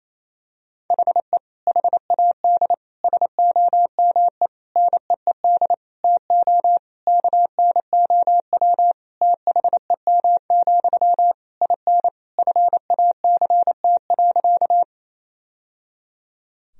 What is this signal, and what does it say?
Morse code, heard as a keyed tone, about 27 wpm, 700 Hz. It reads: HE HAD SOME NEED TO KNOW THEM, IN FACT.
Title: Sentences from Top 1000 Words - Review Part 3 - 28wpm